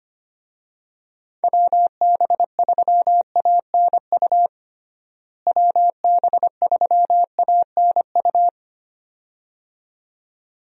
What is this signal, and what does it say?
Morse code, heard as a keyed tone, about 25 wpm, 700 Hz. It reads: WB3ANU WB3ANU
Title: USA Call Signs - 6 Characters - Repeated 2x - 25wpm